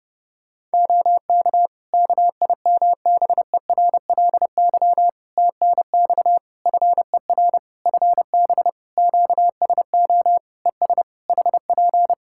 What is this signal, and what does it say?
Morse code, heard as a keyed tone, about 30 wpm, 700 Hz. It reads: OK KIMBERLY TNX FER FB QSO ES HP
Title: Practice QSO - Exchange 4 - Ending - 30wpm